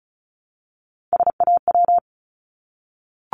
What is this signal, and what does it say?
Morse code, heard as a keyed tone, about 36 wpm, 700 Hz. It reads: SAW
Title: Multi-letter Phonemes Repeated 3x 35wpm